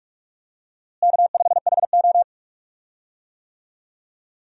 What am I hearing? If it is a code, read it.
Morse code, heard as a keyed tone, about 45 wpm, 700 Hz. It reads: K5HO
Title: USA Call Signs - 4 Characters - Repeated 3x - 45wpm